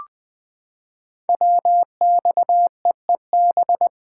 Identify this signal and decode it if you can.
Morse code, heard as a keyed tone, about 20 wpm, 700 Hz. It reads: WXEEB
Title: Code Groups - 5 Letters - 20wpm